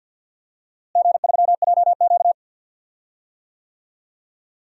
Morse code, heard as a keyed tone, about 49 wpm, 700 Hz. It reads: G3JQ
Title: DXCC-100 Call Signs - 4 Characters - Repeated 3x - 50wpm